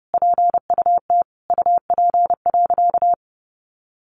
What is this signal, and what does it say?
Morse code, heard as a keyed tone, about 30 wpm, 700 Hz. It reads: PUT UP.
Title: A Sound Of Thunder 30wpm